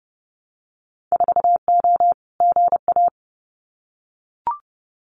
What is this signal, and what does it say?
Morse code, heard as a keyed tone, about 30 wpm, 700 Hz. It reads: 4O GA
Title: Winter Field Day Exchanges 30wpm